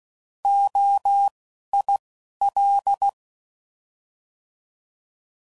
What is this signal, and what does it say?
Morse code, heard as a keyed tone, about 16 wpm, spaced as for 11 wpm, 800 Hz. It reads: OIL
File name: Prefixes / By First Letter / Os